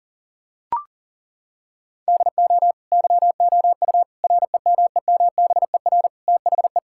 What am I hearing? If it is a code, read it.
Morse code, heard as a keyed tone, about 40 wpm, 700 Hz. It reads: DO YOU REMEMBER THE
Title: Sentences from Top 1600 Words 40wpm